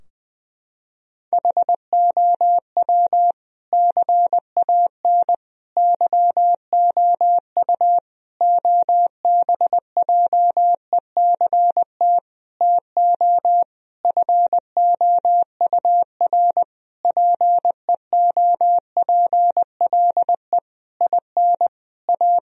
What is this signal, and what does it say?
Morse code, heard as a keyed tone, about 21 wpm, 700 Hz. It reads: HOW CAN YOU OBJECT TO FOUR PEOPLE IN A
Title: Sentences from Top 1000 Words - Review Part 3 - 20wpm